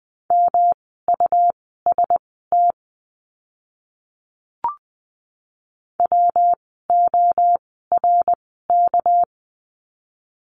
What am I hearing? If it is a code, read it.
Morse code, heard as a keyed tone, about 20 wpm, spaced as for 14 wpm, 700 Hz. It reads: MUST WORK
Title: 237 Lesson 38f - Letter q - Words 20wpm